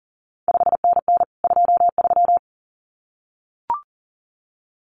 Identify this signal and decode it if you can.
Morse code, heard as a keyed tone, about 41 wpm, 700 Hz. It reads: 5NN 23
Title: CQ WW DX Contest Exchange Encore - 40wpm